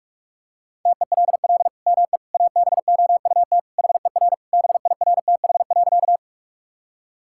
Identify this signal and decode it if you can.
Morse code, heard as a keyed tone, about 41 wpm, 700 Hz. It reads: TELL ME ABOUT HER BIRTH.
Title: Sentences from Top 1000 Words - Review Part 5 - 45wpm